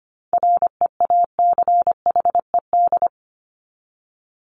Morse code, heard as a keyed tone, about 25 wpm, 700 Hz. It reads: REACHED
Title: ED Suffix - Top 100 Words with Base Words 25wpm